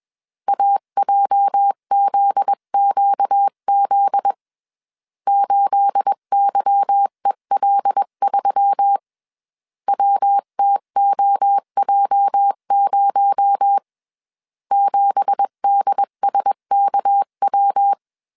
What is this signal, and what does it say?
Morse code, heard as a keyed tone, about 21 wpm, 800 Hz. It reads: AJZQZ 8YEL3 WTOJ0 7DSKW
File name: CW / DET / 1 9 1500C S B 20wpm 2313